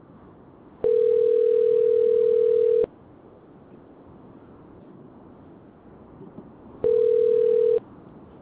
Ringback tone (North America), 440 + 480 Hz.